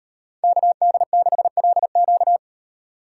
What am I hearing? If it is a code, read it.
Morse code, heard as a keyed tone, about 36 wpm, 700 Hz. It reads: KD6LQ